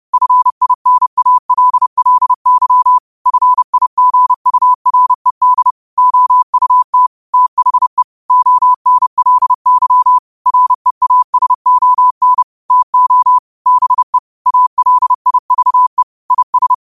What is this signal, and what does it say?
Morse code, heard as a keyed tone, about 30 wpm, 1000 Hz. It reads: FINALLY FIGURED OUT THE ONLY REASON TO BE ALIVE IS
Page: New quotes every day in morse code at 30 Words per minute.